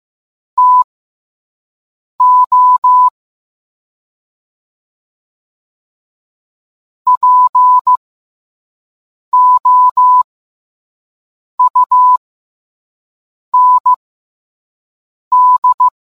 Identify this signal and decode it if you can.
Morse code, heard as a keyed tone, about 15 wpm, spaced as for 5 wpm, 1000 Hz. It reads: TO POUND